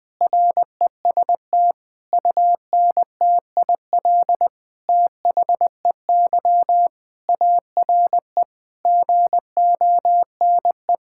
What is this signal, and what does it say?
Morse code, heard as a keyed tone, about 20 wpm, 700 Hz. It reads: REST UNTIL THEY ARE GONE
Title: Sentences from Top 1600 Words 20wpm